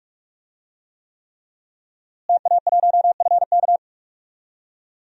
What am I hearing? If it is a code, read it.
Morse code, heard as a keyed tone, about 42 wpm, 700 Hz. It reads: TA1FK
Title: DXCC-100 Call Signs - 5 Characters - Repeated 3x - 45wpm